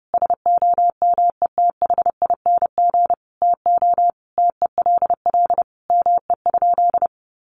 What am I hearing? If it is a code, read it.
Morse code, heard as a keyed tone, about 30 wpm, 700 Hz. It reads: SOMETHING TO TELL ME?
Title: Sentences from Top 600 Words 30wpm